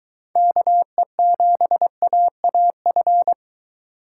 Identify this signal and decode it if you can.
Morse code, heard as a keyed tone, about 22 wpm, 700 Hz. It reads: KE7AAF